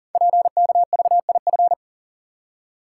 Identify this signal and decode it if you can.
Morse code, heard as a keyed tone, about 42 wpm, 690 Hz. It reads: PKVIF
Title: Code Groups - 5 Letters - 40wpm